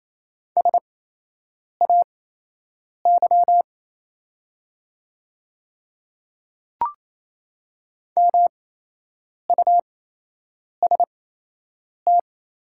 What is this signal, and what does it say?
Morse code, heard as a keyed tone, about 28 wpm, spaced as for 8 wpm, 700 Hz. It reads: SAY MUST
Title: 8x Character Spacing - Top 300 Words - 28wpm